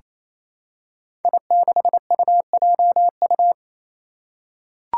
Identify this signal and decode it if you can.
Morse code, heard as a keyed tone, about 27 wpm, 700 Hz. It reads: I6UJU
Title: Five Challenging Letter-Number Combinations 28wpm